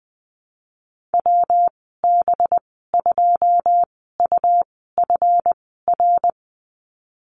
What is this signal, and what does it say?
Morse code, heard as a keyed tone, about 19 wpm, spaced as for 14 wpm, 700 Hz. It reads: WB2UFR
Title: 186 Lesson 30g - Letter b - Call Signs 20wpm